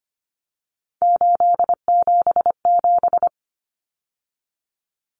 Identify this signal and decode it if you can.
Morse code, heard as a keyed tone, about 25 wpm, 700 Hz. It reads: 877
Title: Common Cut Numbers - 3-Digit 25wpm